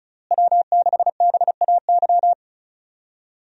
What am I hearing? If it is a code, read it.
Morse code, heard as a keyed tone, about 36 wpm, 700 Hz. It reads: W6BAY
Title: USA Call Signs - 5 Characters - Repeated 2x - 35wpm